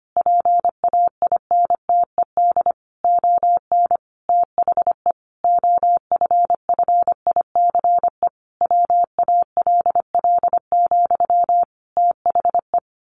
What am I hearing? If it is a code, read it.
Morse code, heard as a keyed tone, about 25 wpm, 700 Hz. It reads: PAINTED ON THE OFFICE WALL, THE
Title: A Sound Of Thunder 25wpm